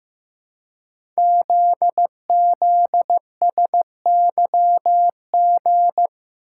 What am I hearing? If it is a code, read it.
Morse code, heard as a keyed tone, about 15 wpm, 700 Hz. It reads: ZZSYG
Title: Code Groups - 5 Letters - 15wpm